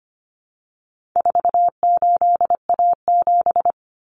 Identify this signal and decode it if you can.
Morse code, heard as a keyed tone, about 25 wpm, 700 Hz. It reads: 48A7
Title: Common Cut Numbers - 4-Digit 25wpm